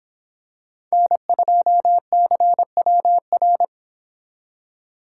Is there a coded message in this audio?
Yes: Morse code from 0.9 to 3.6 s, heard as a keyed tone, about 26 wpm, 700 Hz. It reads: N2CWR